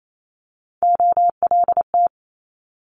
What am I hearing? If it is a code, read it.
Morse code, heard as a keyed tone, about 28 wpm, 700 Hz. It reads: OLT
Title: Multi-letter Phonemes Quick Fire 28wpm